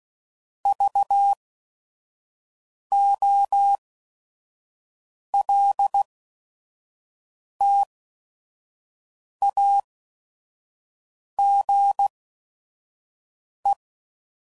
Morse code, heard as a keyed tone, about 16 wpm, spaced as for 5 wpm, 800 Hz. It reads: VOLTAGE
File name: CW Code Groups @5WPM / Technical Subjects / XL